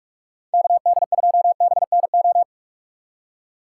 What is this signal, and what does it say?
Morse code, heard as a keyed tone, about 47 wpm, 700 Hz. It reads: KD2BNO